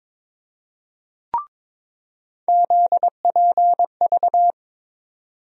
Morse code, heard as a keyed tone, about 22 wpm, 700 Hz. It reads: ZPV
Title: USA Call Sign Postfixes - 3 Letters 22wpm